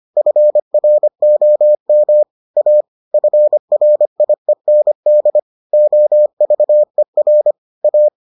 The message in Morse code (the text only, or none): FROM A FRIEND OVER A